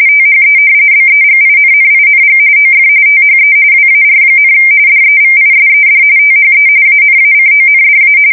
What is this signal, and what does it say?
RTTY, about 45.45 baud. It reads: THE BEGINNING OF THE WAR, DEVISED THE ANACONDA
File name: fsk170 / usnavy3